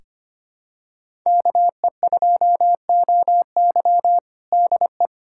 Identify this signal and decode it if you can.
Morse code, heard as a keyed tone, about 25 wpm, 700 Hz. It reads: KE2OY DE
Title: USA Callsign DE Callsign 25wpm